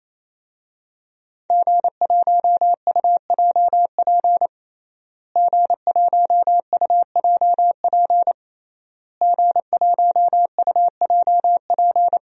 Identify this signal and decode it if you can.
Morse code, heard as a keyed tone, about 27 wpm, 700 Hz. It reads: G1UJP G1UJP G1UJP
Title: DXCC-100 Call Signs - 5 Characters - Repeated 3x - 28wpm